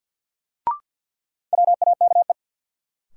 Morse code, heard as a keyed tone, about 50 wpm, 700 Hz. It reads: WAKE
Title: Sets of 4 Words - New Word Drill - 50wpm